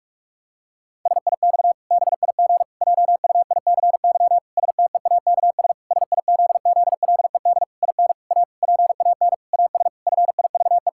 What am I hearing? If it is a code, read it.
Morse code, heard as a keyed tone, about 46 wpm, 700 Hz. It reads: SIX BIG JUICY STEAKS SIZZLED IN A PAN AS FIVE
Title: Pangram Challenge v2 - 45wpm